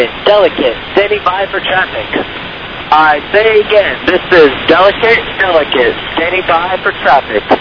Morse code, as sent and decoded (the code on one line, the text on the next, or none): none